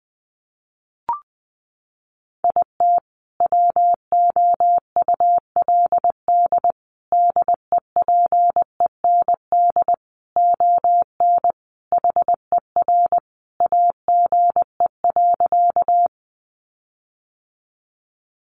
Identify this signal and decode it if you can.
Morse code, heard as a keyed tone, about 20 wpm, 700 Hz. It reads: IT WOULD DEPEND ON HER AGE.
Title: Sentences from Top 1300 Words 20wpm